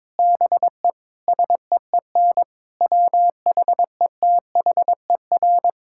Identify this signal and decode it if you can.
Morse code, heard as a keyed tone, about 22 wpm, 700 Hz. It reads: BE SEEN WHETHER
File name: practice / 22wpm / Sets of 6 Words - 22wpm